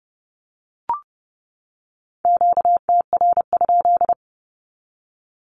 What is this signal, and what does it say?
Morse code, heard as a keyed tone, about 30 wpm, 700 Hz. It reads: QTR?
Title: Q-Codes 30wpm